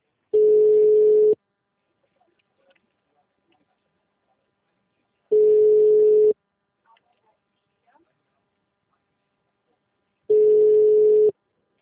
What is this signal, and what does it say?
Ringback tone (Europe, 425 Hz), with speech.
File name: out- -2018- - - .12384